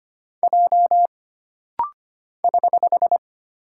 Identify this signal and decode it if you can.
Morse code, heard as a keyed tone, about 25 wpm, 700 Hz. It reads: J <HH>
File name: Single Character (Letter:Number:Punctuation:Prosign) - Rapid-Fire - 25wpm